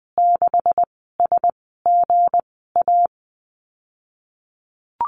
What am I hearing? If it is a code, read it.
Morse code, heard as a keyed tone, about 19 wpm, 700 Hz. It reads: 6 S G A
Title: 179 Lesson 29g - Slash - Call Signs 20wpm